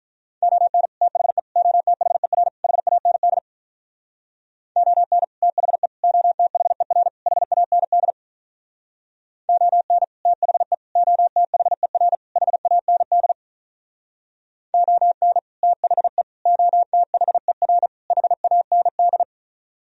Morse code, heard as a keyed tone, about 46 wpm, 700 Hz. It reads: ON THE OTHER HAND ON THE OTHER HAND ON THE OTHER HAND ON THE OTHER HAND